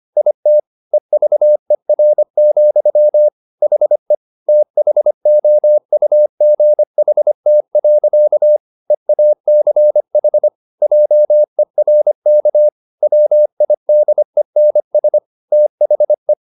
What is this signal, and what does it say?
Morse code, heard as a keyed tone, about 25 wpm, 600 Hz. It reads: IT EVER, HE THOUGHT. EACH JERK WIDENS THE
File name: the old man and the sea part 1 25wpm